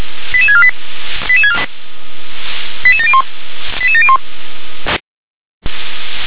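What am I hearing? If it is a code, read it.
Five-tone selective call (ZVEI), heard as quick runs of five tones, tones of about 80 ms. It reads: 88648 88701 88701
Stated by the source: ZVEI five-tone selective call received on an SDR.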